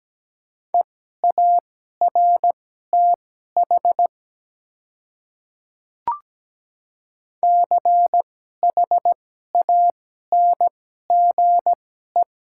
Morse code, heard as a keyed tone, about 17 wpm, spaced as for 12 wpm, 700 Hz. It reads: EARTH CHANGE
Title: 2x Character Spacing - Top 200 Words - 17wpm